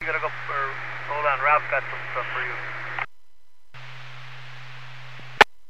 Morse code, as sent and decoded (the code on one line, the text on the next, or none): none